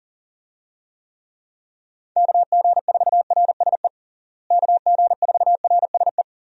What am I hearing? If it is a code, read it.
Morse code, heard as a keyed tone, about 40 wpm, 700 Hz. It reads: KG4RSE KG4RSE
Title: USA Call Signs - 6 Characters - Repeated 2x - 40wpm